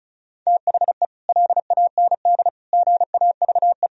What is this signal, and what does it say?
Morse code, heard as a keyed tone, about 34 wpm, 700 Hz. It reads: THE LAND GAVE
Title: Sentences from Top 1000 Words - Review Part 1 - 35wpm